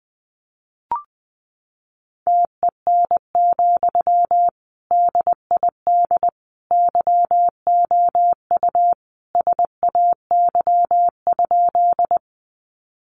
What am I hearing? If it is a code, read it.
Morse code, heard as a keyed tone, about 20 wpm, 700 Hz. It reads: TEN, DID YOU SAY?
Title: Sentences from Top 400 Words 20wpm